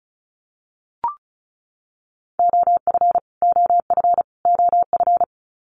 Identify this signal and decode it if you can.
Morse code, heard as a keyed tone, about 34 wpm, 700 Hz. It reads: OF OF OF
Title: Top 500 Words Repeated 35wpm